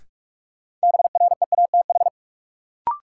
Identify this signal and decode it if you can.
Morse code, heard as a keyed tone, about 46 wpm, 700 Hz. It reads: BREATH
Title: Sets of 5 Words Encore - New Word Drill - 45wpm